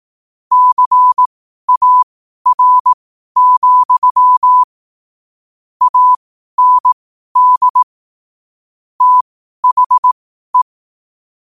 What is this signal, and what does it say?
Morse code, heard as a keyed tone, about 18 wpm, spaced as for 12 wpm, 1000 Hz. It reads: CAR, AND THE
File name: qotdfiles / 12wpm